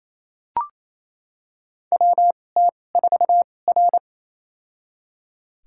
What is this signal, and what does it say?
Morse code, heard as a keyed tone, about 28 wpm, spaced as for 20 wpm, 700 Hz. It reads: WT4R
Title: USA Call Signs - 4 Characters - 2x Character Spacing - 28wpm